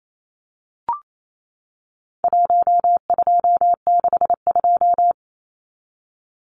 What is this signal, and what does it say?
Morse code, heard as a keyed tone, about 28 wpm, 700 Hz. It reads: 1262
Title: Numbers - 4-Digit 28wpm